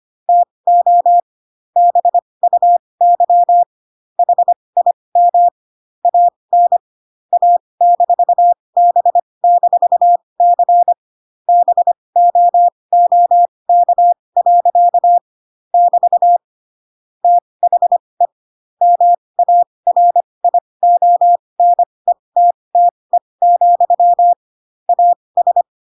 Morse code, heard as a keyed tone, about 25 wpm, spaced as for 20 wpm, 700 Hz. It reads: TO BUY HIM AN A-B-C BOOK. <BT> THE MARIONETTE, AS
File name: Pinocchio / Pinocchio ch8 15 25 ik1hge